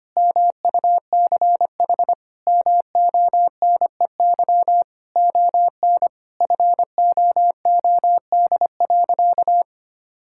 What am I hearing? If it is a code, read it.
Morse code, heard as a keyed tone, about 25 wpm, 700 Hz. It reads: MUCH MONEY ON FOOD.
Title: Sentences from Top 1000 Words - Review Part 5 - 25wpm